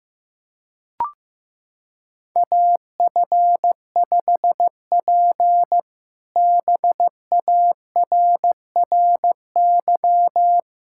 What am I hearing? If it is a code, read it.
Morse code, heard as a keyed tone, about 15 wpm, 700 Hz. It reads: AF5P BARRY
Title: K1USN SST QSO Practice - 15wpm